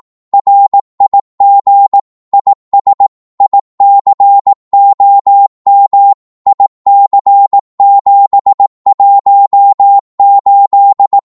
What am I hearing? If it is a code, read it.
Morse code, heard as a keyed tone, about 18 wpm, 800 Hz. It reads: RIG IS ICOM IC718
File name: Morse / qso-18wpm